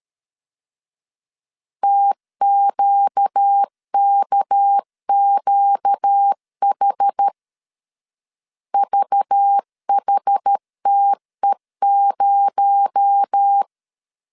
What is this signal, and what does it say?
Morse code, heard as a keyed tone, about 13 wpm, 800 Hz. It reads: TQKQH VHTE0
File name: CW / DET / 1 6 1500C S A 12wpm 3440